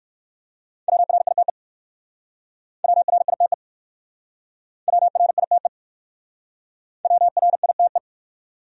Morse code, heard as a keyed tone, about 60 wpm, 700 Hz. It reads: WRITE WRITE WRITE WRITE